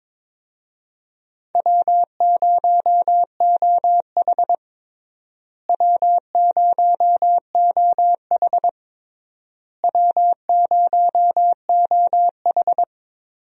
Morse code, heard as a keyed tone, about 22 wpm, 700 Hz. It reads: W0OH W0OH W0OH
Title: USA Call Signs - 4 Characters - Repeated 3x - 22wpm